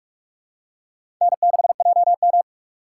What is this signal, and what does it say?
Morse code, heard as a keyed tone, about 46 wpm, 700 Hz. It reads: N6JM
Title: USA Call Signs - 4 Characters - 4x Character Spacing - 45wpm